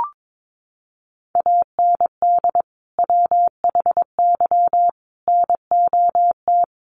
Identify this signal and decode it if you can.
Morse code, heard as a keyed tone, about 22 wpm, 700 Hz. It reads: AND WHY NOT
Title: Sentences from Top 1300 Words 22wpm